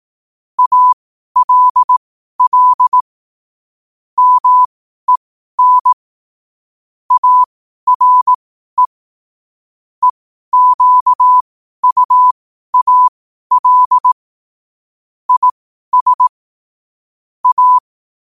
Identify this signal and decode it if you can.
Morse code, heard as a keyed tone, about 18 wpm, spaced as for 12 wpm, 1000 Hz. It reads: ALL MEN ARE EQUAL IS A
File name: qotdfiles / 12wpm